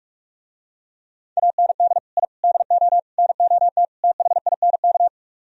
Morse code, heard as a keyed tone, about 48 wpm, 700 Hz. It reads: AND I DO NOT THINK